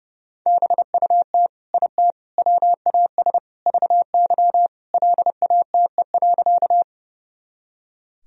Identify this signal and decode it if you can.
Morse code, heard as a keyed tone, about 32 wpm, 700 Hz. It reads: BUT IT WAS VY LATE.
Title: Sentences from Top 300 Words and Abbreviations 30wpm